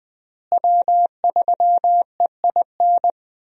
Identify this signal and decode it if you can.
Morse code, heard as a keyed tone, about 20 wpm, 700 Hz. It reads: W3EIN
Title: USA Call Signs - 5 Characters - Repeated 3x - 20wpm